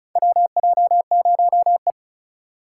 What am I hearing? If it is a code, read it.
Morse code, heard as a keyed tone, about 36 wpm, 700 Hz. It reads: WJ0E
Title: USA Call Signs - 4 Characters - 2x Character Spacing - 35wpm